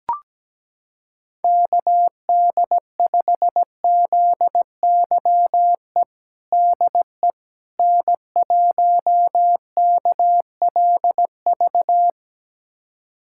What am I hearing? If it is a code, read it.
Morse code, heard as a keyed tone, about 17 wpm, 700 Hz. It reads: KD5ZYE DE N1KLV
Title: USA Callsign DE Callsign 17wpm